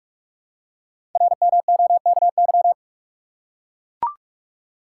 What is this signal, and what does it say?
Morse code, heard as a keyed tone, about 44 wpm, 700 Hz. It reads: RMOKY
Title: Code Groups - 5 Letters - 45wpm